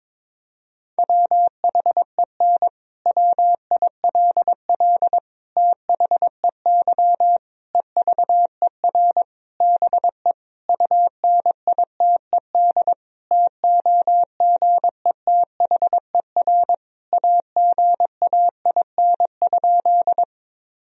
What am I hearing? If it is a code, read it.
Morse code, heard as a keyed tone, about 22 wpm, 700 Hz. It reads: WHEN WILL THEY EVER BE UNITED TOGETHER AGAIN?
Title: Sentences from Top 1000 Words - Review Part 4 - 22wpm